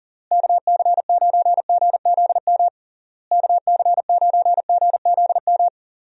Morse code, heard as a keyed tone, about 40 wpm, 700 Hz. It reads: KC9GZM KC9GZM
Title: USA Call Signs - 6 Characters - Repeated 3x - 40wpm